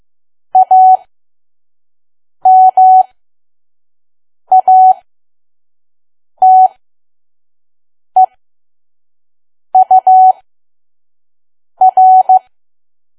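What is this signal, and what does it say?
Morse code, heard as a keyed tone, about 15 wpm, spaced as for 5 wpm, 750 Hz. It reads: AMATEUR
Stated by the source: Morse code practice at 5 WPM from KQ2P.COM.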